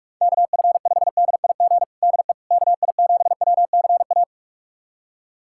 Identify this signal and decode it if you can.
Morse code, heard as a keyed tone, about 43 wpm, 700 Hz. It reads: KF5DIG DE KI7WCA
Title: USA Callsign DE Callsign 45wpm